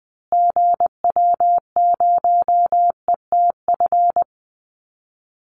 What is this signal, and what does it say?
Morse code, heard as a keyed tone, about 20 wpm, 700 Hz. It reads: GW0ETF
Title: Call Signs Repeated 2x - 20wpm